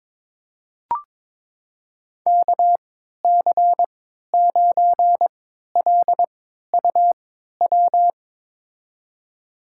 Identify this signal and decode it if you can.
Morse code, heard as a keyed tone, about 22 wpm, spaced as for 13 wpm, 700 Hz. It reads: KC9LUW
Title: USA Call Signs - 6 Characters - 3x Character Spacing - 22wpm